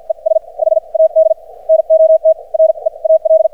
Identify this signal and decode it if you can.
Morse code, heard as a keyed tone, about 51 wpm, spaced as for 39 wpm, 590 Hz. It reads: ES HAD NOT REAL
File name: archived / CRSnet